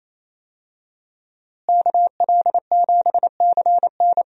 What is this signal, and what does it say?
Morse code, heard as a keyed tone, about 28 wpm, 700 Hz. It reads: KL7CN
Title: USA Call Signs - 5 Characters - Repeated 2x - 28wpm